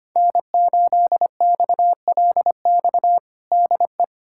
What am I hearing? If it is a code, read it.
Morse code, heard as a keyed tone, about 25 wpm, 700 Hz. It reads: N8XLX DE
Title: USA Callsign DE Callsign - 5 Characters - 25wpm